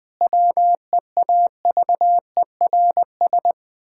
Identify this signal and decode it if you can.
Morse code, heard as a keyed tone, about 20 wpm, 700 Hz. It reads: WEAVERS
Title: Pangram Challenge v2 - 20wpm